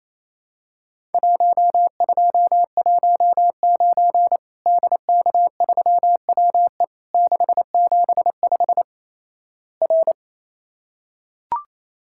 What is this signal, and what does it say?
Morse code, heard as a keyed tone, about 28 wpm, 700 Hz. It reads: 1219 DK3WE 675 R
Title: Worked All Europe DX Contest - QTC Exchange - 28wpm